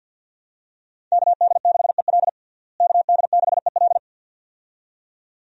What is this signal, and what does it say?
Morse code, heard as a keyed tone, about 48 wpm, 700 Hz. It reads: KD6EL KD6EL
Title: USA Call Signs - 5 Characters - Repeated 2x - 50wpm